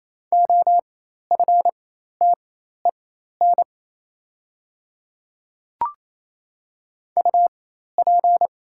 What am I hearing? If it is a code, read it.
Morse code, heard as a keyed tone, about 28 wpm, spaced as for 13 wpm, 700 Hz. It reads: OFTEN UP
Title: 4x Character Spacing - Top 300 Words - 28wpm